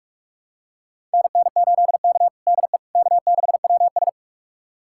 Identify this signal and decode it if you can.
Morse code, heard as a keyed tone, about 42 wpm, 700 Hz. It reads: NN8K DE K6WS
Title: USA Callsign DE Callsign - 4 Characters - 45wpm